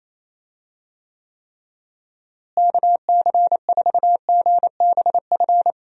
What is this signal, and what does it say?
Morse code, heard as a keyed tone, about 28 wpm, 700 Hz. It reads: KC4GBF
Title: USA Call Signs - 6 Characters - Repeated 3x - 28wpm